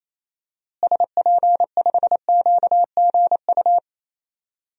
Morse code, heard as a keyed tone, about 28 wpm, 700 Hz. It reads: SP5QGU
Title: DXCC-100 Call Signs - 6 Characters - Repeated 3x - 28wpm